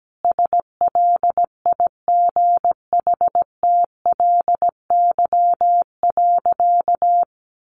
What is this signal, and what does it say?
Morse code, heard as a keyed tone, about 17 wpm, 700 Hz. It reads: SLIGHTLY.
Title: Sentences from Top 1300 Words 17wpm